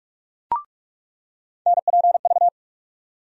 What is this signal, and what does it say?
Morse code, heard as a keyed tone, about 45 wpm, 700 Hz. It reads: NPV
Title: USA Call Sign Postfixes - 3 Letters 45wpm